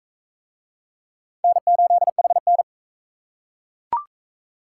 Morse code, heard as a keyed tone, about 39 wpm, 700 Hz. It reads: N8HN